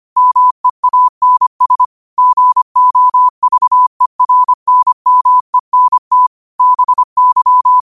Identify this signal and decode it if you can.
Morse code, heard as a keyed tone, about 25 wpm, 1000 Hz. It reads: MEANS GOVERNMENT BY
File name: qotdfiles / 25wpm